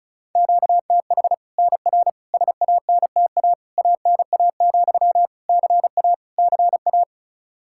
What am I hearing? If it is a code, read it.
Morse code, heard as a keyed tone, about 34 wpm, 700 Hz. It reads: QTH NR SANTA ANA, CA CA
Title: Practice QSO - Exchange 1 - Beginning - 35wpm